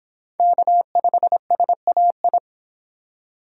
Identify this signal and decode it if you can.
Morse code, heard as a keyed tone, about 26 wpm, 700 Hz. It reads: K5SAI